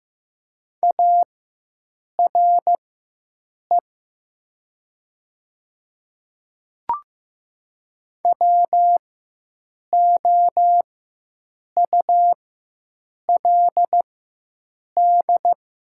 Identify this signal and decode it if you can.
Morse code, heard as a keyed tone, about 15 wpm, spaced as for 7 wpm, 700 Hz. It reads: ARE WOULD
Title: 4x Character Spacing - Top 100 Words - 15wpm